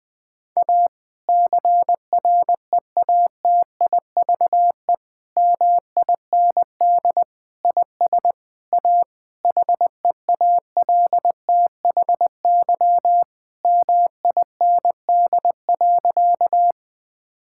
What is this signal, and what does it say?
Morse code, heard as a keyed tone, about 21 wpm, 700 Hz. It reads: A CREATIVE MIND IS A HEALTHY MIND.
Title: Sentences from Top 1000 Words - Review Part 5 - 20wpm